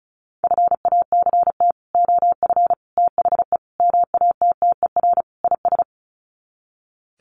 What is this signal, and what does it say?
Morse code, heard as a keyed tone, about 34 wpm, 700 Hz. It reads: FACT OF THE MATTER IS
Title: Sets of 5 Words 35wpm